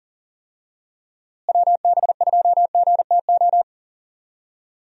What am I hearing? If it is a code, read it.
Morse code, heard as a keyed tone, about 40 wpm, 700 Hz. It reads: WB2GTO
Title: USA Call Signs - 6 Characters - Repeated 3x - 40wpm